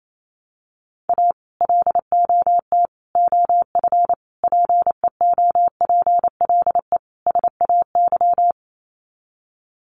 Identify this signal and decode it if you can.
Morse code, heard as a keyed tone, about 28 wpm, 700 Hz. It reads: A LOT OF PEOPLE SAY
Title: Sets of 5 Words Encore2 28wpm